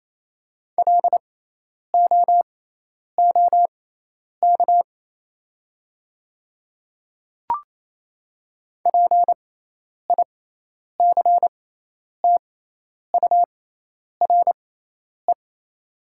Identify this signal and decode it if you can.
Morse code, heard as a keyed tone, about 28 wpm, spaced as for 10 wpm, 700 Hz. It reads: LOOK PICTURE
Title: 6x Character Spacing - Top 200 Words - 28wpm